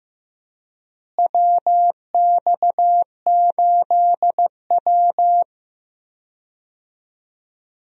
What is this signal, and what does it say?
Morse code, heard as a keyed tone, about 15 wpm, 700 Hz. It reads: WX8W
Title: USA Call Signs - 4 Characters - Repeated 3x - 15wpm